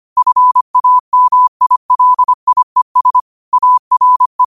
Morse code, heard as a keyed tone, about 25 wpm, 1000 Hz. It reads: FAMILIES ARE
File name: qotdfiles / 25wpm